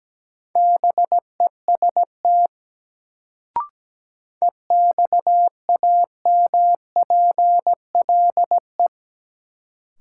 Morse code, heard as a keyed tone, about 17 wpm, 700 Hz. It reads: BEST EXAMPLE
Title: Sentences from Top 400 Words - New Word Drill - 17wpm